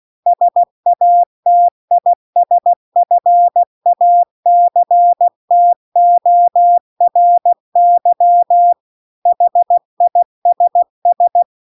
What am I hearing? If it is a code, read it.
Morse code, heard as a keyed tone, about 16 wpm, 700 Hz. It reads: SATISFACTORY HISS